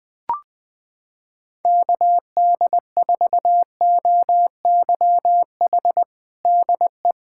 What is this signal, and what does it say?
Morse code, heard as a keyed tone, about 21 wpm, 700 Hz. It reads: KD4OYH DE
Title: USA Callsign DE Callsign - 6 Characters - 20wpm